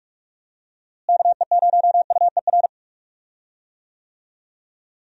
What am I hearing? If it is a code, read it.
Morse code, heard as a keyed tone, about 46 wpm, 700 Hz. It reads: KE0UER
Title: USA Call Signs - 6 Characters - Repeated 3x - 45wpm